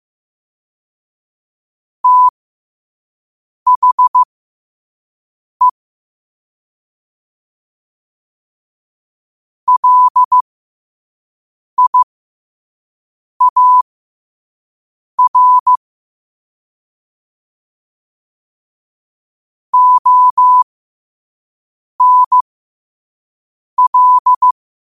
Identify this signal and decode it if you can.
Morse code, heard as a keyed tone, about 15 wpm, spaced as for 5 wpm, 1000 Hz. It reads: THE LIAR ONL
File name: qotdfiles / 5wpm